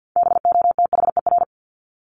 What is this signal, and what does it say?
Morse code, heard as a keyed tone, about 48 wpm, 700 Hz. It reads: BOTHER
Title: Top 1501-1750 Words 50wpm